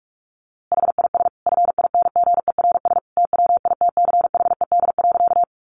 Morse code, heard as a keyed tone, about 44 wpm, 700 Hz. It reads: HIS FINGERS TWITCHED.
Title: A Sound Of Thunder 45wpm